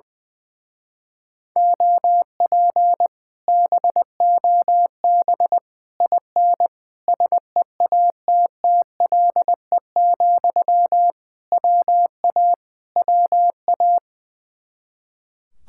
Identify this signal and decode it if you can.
Morse code, heard as a keyed tone, about 20 wpm, 700 Hz. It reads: OP BOB IN SEATTLE, WA WA
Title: Speed-Racing Common QSO Phrases - 20wpm